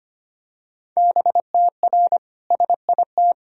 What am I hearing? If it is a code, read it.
Morse code, heard as a keyed tone, about 25 wpm, 700 Hz. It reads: BTR SIT
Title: Sentences from Top 500 Words and Abbreviations 25wpm